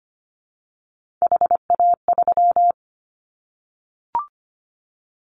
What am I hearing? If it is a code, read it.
Morse code, heard as a keyed tone, about 25 wpm, 700 Hz. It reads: HA3
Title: Callsign Prefixes for Top 25 DXCC - Repeated 3x - 25wpm